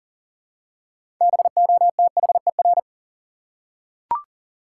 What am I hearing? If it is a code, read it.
Morse code, heard as a keyed tone, about 39 wpm, 700 Hz. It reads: BOTHER
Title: Six Letter Words - 40wpm